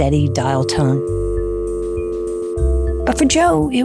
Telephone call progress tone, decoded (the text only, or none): dial tone